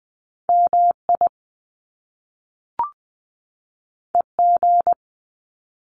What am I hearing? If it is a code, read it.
Morse code, heard as a keyed tone, about 20 wpm, 700 Hz. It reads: MI EG
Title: Top Two Letter Combinations in Words 20wpm